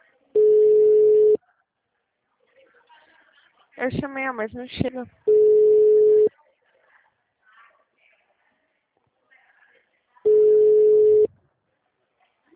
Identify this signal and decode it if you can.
Ringback tone (Europe, 425 Hz), with speech.
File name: out- -2013- - - .83863